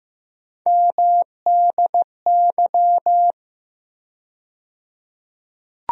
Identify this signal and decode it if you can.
Morse code, heard as a keyed tone, about 15 wpm, 700 Hz. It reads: MDY
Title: USA Call Sign Postfixes - 3 Letters 15wpm